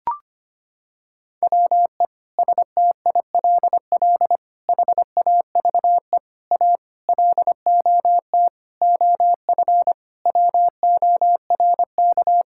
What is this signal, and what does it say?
Morse code, heard as a keyed tone, about 25 wpm, 700 Hz. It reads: WE STILL HAVE A LOT OF WORK
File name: Sets of 9 Words - 25wpm